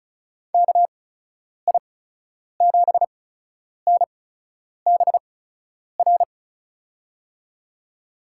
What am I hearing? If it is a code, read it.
Morse code, heard as a keyed tone, about 36 wpm, spaced as for 10 wpm, 700 Hz. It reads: KI7NBR
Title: USA Call Signs - 6 Characters - 8x Character Spacing - 35wpm